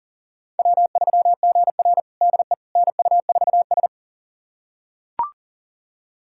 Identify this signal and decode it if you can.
Morse code, heard as a keyed tone, about 40 wpm, 700 Hz. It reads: W3GR DE NU4S
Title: USA Callsign DE Callsign - 4 Characters - 40wpm